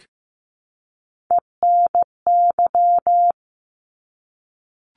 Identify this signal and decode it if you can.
Morse code, heard as a keyed tone, about 15 wpm, 700 Hz. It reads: ENY
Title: ARRL Sections 15wpm